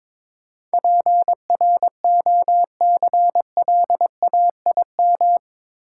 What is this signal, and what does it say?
Morse code, heard as a keyed tone, about 22 wpm, 700 Hz. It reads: PROCLAIM
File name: practice / 22wpm / Eight Letter Words - 22wpm